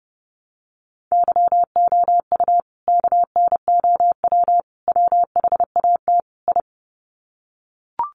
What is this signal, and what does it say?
Morse code, heard as a keyed tone, about 30 wpm, 700 Hz. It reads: YOU KNOW WHAT I
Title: Sets of 4 Words 30wpm